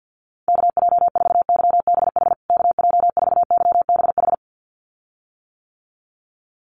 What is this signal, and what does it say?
Morse code, heard as a keyed tone, about 49 wpm, 700 Hz. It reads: KJ4YBH KJ4YBH
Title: USA Call Signs Repeated 2x 50wpm